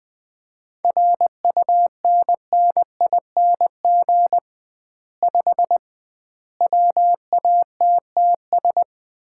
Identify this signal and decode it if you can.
Morse code, heard as a keyed tone, about 21 wpm, 700 Hz. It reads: RUNNING 5 WATTS
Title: Common QSO Phrases - 2x Word-Spacing - 20wpm